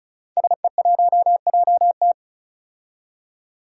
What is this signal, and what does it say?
Morse code, heard as a keyed tone, about 36 wpm, 700 Hz. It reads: SE1JT
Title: Five Challenging Letter-Number Combinations 35wpm